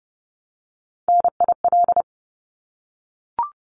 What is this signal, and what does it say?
Morse code, heard as a keyed tone, about 30 wpm, 700 Hz. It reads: NIL
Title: Abbreviations Repeated 3x - 30wpm